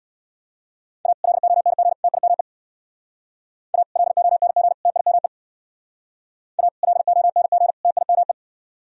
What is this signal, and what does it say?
Morse code, heard as a keyed tone, about 61 wpm, 700 Hz. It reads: A LONG TIME A LONG TIME A LONG TIME
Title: Speed-Racing - Sets of 3 Words 50wpm